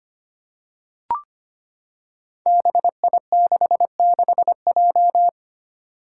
Morse code, heard as a keyed tone, about 25 wpm, 700 Hz. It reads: BI66J
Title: Five Challenging Letter-Number Combinations 25wpm